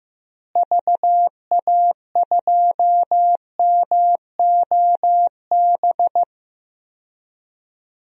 Morse code, heard as a keyed tone, about 15 wpm, 700 Hz. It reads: VA2MOB
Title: DXCC-100 Call Signs - 6 Characters - Repeated 3x - 15wpm